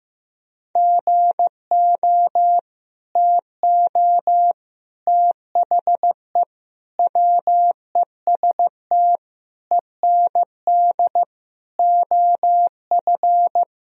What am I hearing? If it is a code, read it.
Morse code, heard as a keyed tone, about 15 wpm, 700 Hz. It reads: GO TO THE WEST END OF
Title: Sentences from Top 1000 Words - Review Part 2 - 15wpm